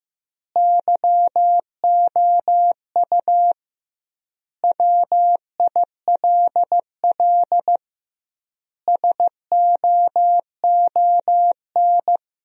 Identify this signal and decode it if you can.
Morse code, heard as a keyed tone, about 15 wpm, 700 Hz. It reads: YOU WILL SOON
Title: 2x Word-Spacing - Sentences from Top 400 Words - 15wpm